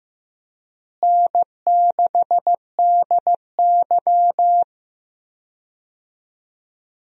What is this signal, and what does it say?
Morse code, heard as a keyed tone, about 15 wpm, 700 Hz. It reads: N6DY
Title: USA Call Signs - 4 Characters - Repeated 3x - 15wpm